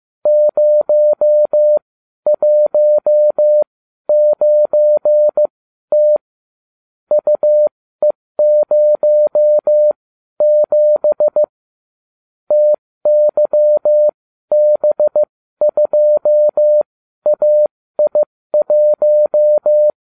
Morse code, heard as a keyed tone, about 15 wpm, spaced as for 12 wpm, 600 Hz. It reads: 019T UE07 TYB2AI1